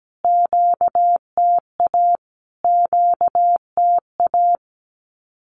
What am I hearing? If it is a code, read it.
Morse code, heard as a keyed tone, about 17 wpm, 700 Hz. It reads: QTA QTA
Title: Q-Codes Repeated 3x 17wpm